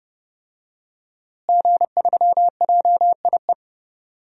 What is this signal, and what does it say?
Morse code, heard as a keyed tone, about 30 wpm, 700 Hz. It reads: G3JIE
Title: DXCC-100 Call Signs - 5 Characters - Repeated 3x - 30wpm